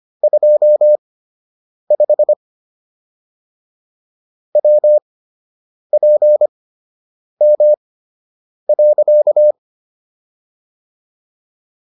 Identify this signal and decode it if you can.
Morse code, heard as a keyed tone, about 25 wpm, spaced as for 8 wpm, 600 Hz. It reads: 25 WPM.